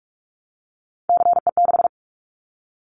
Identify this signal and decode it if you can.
Morse code, heard as a keyed tone, about 44 wpm, 700 Hz. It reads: CE6
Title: Callsign Prefixes for Top 100 DXCC - 45wpm